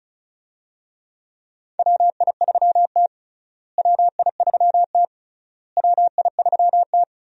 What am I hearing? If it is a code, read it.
Morse code, heard as a keyed tone, about 36 wpm, 700 Hz. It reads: WI3T WI3T WI3T
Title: USA Call Signs - 4 Characters - Repeated 3x - 35wpm